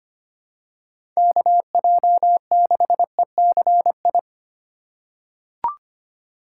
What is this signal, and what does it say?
Morse code, heard as a keyed tone, about 25 wpm, 700 Hz. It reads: KJ6ECI
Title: USA Call Signs - 6 Characters - 4x Character Spacing - 25wpm